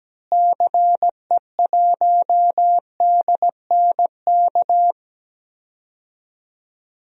Morse code, heard as a keyed tone, about 17 wpm, 700 Hz. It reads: CE1DNK
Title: DXCC-100 Call Signs - 6 Characters - Repeated 2x - 17wpm